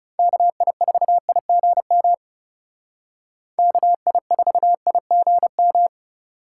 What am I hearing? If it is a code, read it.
Morse code, heard as a keyed tone, about 38 wpm, 700 Hz. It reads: KI4IGM KI4IGM